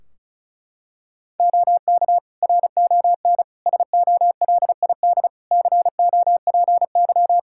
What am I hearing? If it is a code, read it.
Morse code, heard as a keyed tone, about 34 wpm, 700 Hz. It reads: OK RON SOLID COPY
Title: Practice QSO - Exchange 3 - More Info - 35wpm